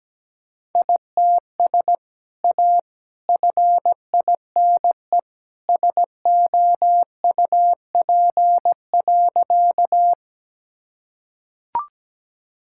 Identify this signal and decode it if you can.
Morse code, heard as a keyed tone, about 17 wpm, 700 Hz. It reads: ITS A FINE SOUP.
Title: Sentences from Top 1000 Words - Review Part 6 - 17wpm